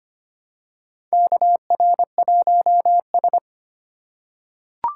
Morse code, heard as a keyed tone, about 25 wpm, 700 Hz. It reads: KR1S
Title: USA Call Signs - 4 Characters - 2x Character Spacing - 25wpm